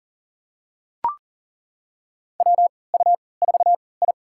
Morse code, heard as a keyed tone, about 40 wpm, spaced as for 23 wpm, 700 Hz. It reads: WU4I
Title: USA Call Signs - 4 Characters - 3x Character Spacing - 40wpm